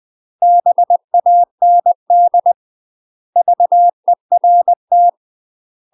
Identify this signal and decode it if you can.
Morse code, heard as a keyed tone, about 20 wpm, 700 Hz. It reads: BAND VERT
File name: UK MASTER QSOs TRAINING / 20WPM / UK MASTER QSOs 0068